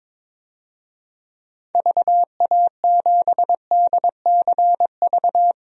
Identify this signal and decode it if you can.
Morse code, heard as a keyed tone, about 21 wpm, 700 Hz. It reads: VA7DCV
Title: DXCC-100 Call Signs - 6 Characters - 22wpm